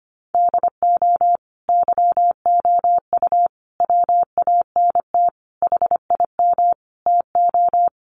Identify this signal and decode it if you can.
Morse code, heard as a keyed tone, about 25 wpm, 700 Hz. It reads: DO YOU WANT HIM TO
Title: Sentences from Top 1400 Words 25wpm